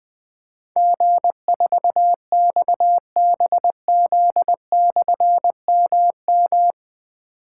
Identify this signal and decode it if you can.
Morse code, heard as a keyed tone, about 20 wpm, 700 Hz. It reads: G4XBZ/MM
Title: Difficult Call Signs 20wpm